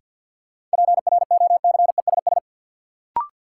Morse code, heard as a keyed tone, about 52 wpm, 700 Hz. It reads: PROCESS
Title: Sentences from Top 1000 Words - Review Part 3 - New Word Drill - 50wpm